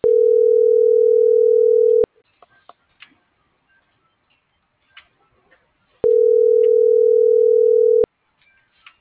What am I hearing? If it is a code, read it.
Ringback tone (North America), 440 + 480 Hz.
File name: CENTRAL NAVIERAS / out janeth.ramirez